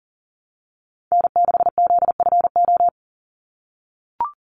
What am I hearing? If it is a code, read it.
Morse code, heard as a keyed tone, about 40 wpm, 700 Hz. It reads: N6ZFO
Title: Call Signs - 40wpm